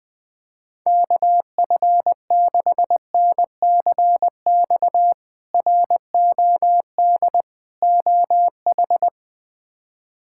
Morse code, heard as a keyed tone, about 20 wpm, 700 Hz. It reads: KF6NCX ROD OH
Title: K1USN SST QSO Practice - 20wpm